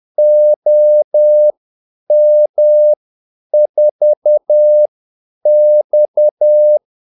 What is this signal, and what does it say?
Morse code, heard as a keyed tone, about 10 wpm, spaced as for 8 wpm, 600 Hz. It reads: OM4X
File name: Callsigns-10-8wpm